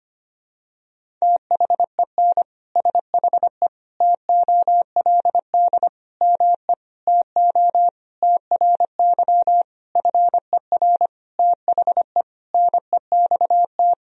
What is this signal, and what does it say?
Morse code, heard as a keyed tone, about 25 wpm, 700 Hz. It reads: THEN SHE TOLD ME TO TRY FER THE NEXT
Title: Sentences from Top 400 Words and Abbreviations 25wpm